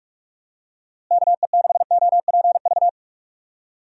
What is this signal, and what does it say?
Morse code, heard as a keyed tone, about 42 wpm, 700 Hz. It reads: KE6OPV
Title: USA Call Signs - 6 Characters - Repeated 2x - 45wpm